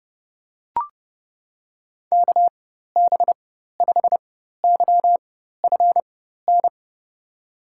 Morse code, heard as a keyed tone, about 31 wpm, spaced as for 14 wpm, 700 Hz. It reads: KB5YFN
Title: USA Call Signs - 6 Characters - 4x Character Spacing - 30wpm